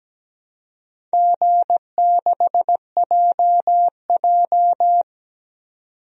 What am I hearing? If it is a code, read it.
Morse code, heard as a keyed tone, about 17 wpm, 700 Hz. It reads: G6JJ
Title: DXCC-100 Call Signs - 4 Characters - 17wpm